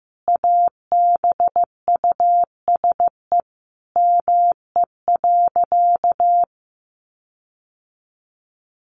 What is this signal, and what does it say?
Morse code, heard as a keyed tone, about 15 wpm, 700 Hz. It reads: ABUSE ME.
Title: Sentences from Top 1300 Words 15wpm